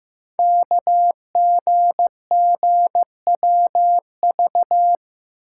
Morse code, heard as a keyed tone, about 15 wpm, 700 Hz. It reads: KGGWV
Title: Code Groups - 5 Letters - 15wpm